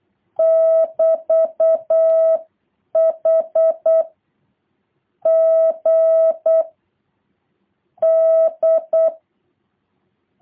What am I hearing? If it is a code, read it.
Morse code, heard as a keyed tone, about 8 wpm, 650 Hz. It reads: <BT>H G D